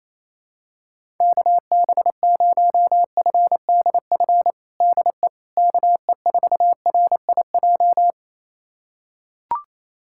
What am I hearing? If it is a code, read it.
Morse code, heard as a keyed tone, about 28 wpm, 700 Hz. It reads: KB0FDF DE KE4RIJ
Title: USA Callsign DE Callsign 28wpm